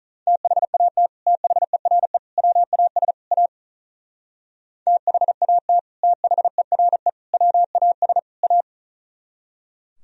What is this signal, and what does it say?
Morse code, heard as a keyed tone, about 44 wpm, 700 Hz. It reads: THAT THERE WAS A THAT THERE WAS A